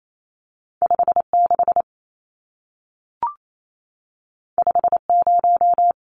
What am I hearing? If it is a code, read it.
Morse code, heard as a keyed tone, about 28 wpm, 700 Hz. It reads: 56 50
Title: Numbers - 2-Digit 28wpm